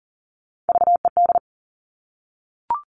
Morse code, heard as a keyed tone, about 40 wpm, 690 Hz. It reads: VED
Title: Top Three Letter Combinations in Words 40wpm